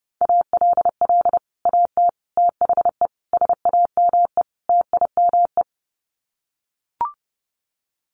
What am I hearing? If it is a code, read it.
Morse code, heard as a keyed tone, about 30 wpm, 700 Hz. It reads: ALL AT THE SAME TIME
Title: Sets of 5 Words Encore2 30wpm